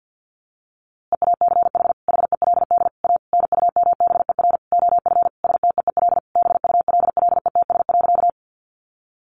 Morse code, heard as a keyed tone, about 50 wpm, 700 Hz. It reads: EACH HELD A NUMBER OF STEEL BULLETS.
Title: A Sound Of Thunder 50wpm